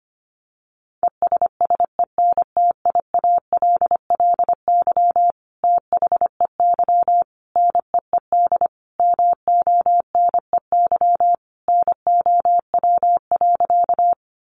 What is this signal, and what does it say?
Morse code, heard as a keyed tone, about 25 wpm, 700 Hz. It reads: ESSENTIALLY THEY NEED MONEY NOW.
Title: Sentences from Top 1300 Words 25wpm